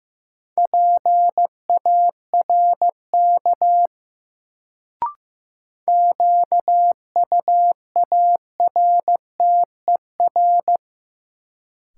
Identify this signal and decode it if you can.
Morse code, heard as a keyed tone, about 15 wpm, 700 Hz. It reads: PARK QUARTER
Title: Sentences from Top 1300 Words - New Word Drill - 15wpm